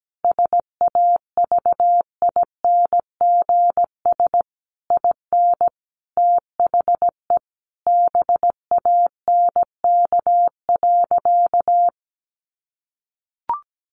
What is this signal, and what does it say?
Morse code, heard as a keyed tone, about 17 wpm, 700 Hz. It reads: SAVINGS IN THE BANK.
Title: Sentences from Top 1200 Words 17wpm